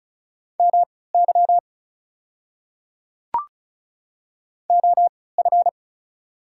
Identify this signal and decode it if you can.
Morse code, heard as a keyed tone, about 36 wpm, spaced as for 20 wpm, 690 Hz. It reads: MY OF
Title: 3x Character Spacing - Top 100 Words - 35wpm